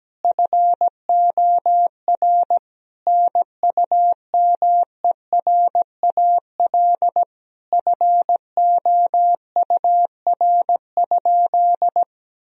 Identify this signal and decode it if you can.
Morse code, heard as a keyed tone, about 17 wpm, 700 Hz. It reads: FOR NUMERAL FOUR?
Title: Sentences from Top 1000 Words - Review Part 2 - 17wpm